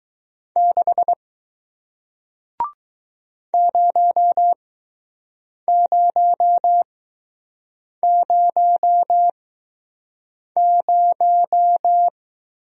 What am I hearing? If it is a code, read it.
Morse code, heard as a keyed tone, about 20 wpm, spaced as for 6 wpm, 700 Hz. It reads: 6 0000